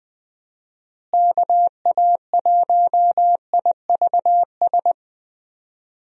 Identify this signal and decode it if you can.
Morse code, heard as a keyed tone, about 21 wpm, 700 Hz. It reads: KA1IVS
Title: USA Call Signs - 6 Characters - Repeated 2x - 20wpm